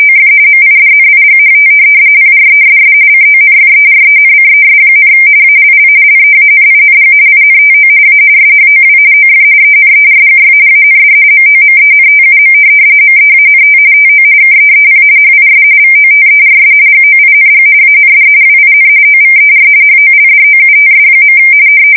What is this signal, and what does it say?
RTTY, about 45.45 baud. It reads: OF MIDNIGHT PRESS CONFERENCES, ALL LEANING INTO A SHOW OF ENERGY AND MOMENTUM MEANT TO KEEP HIS VOTERS GOING AND WIN OVER HOLDOUTS.
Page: 170 Shift 2125/2295